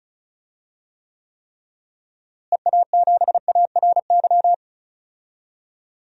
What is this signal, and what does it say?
Morse code, heard as a keyed tone, about 36 wpm, 700 Hz. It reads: EA7ARY
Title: DXCC-100 Call Signs - 6 Characters - Repeated 3x - 35wpm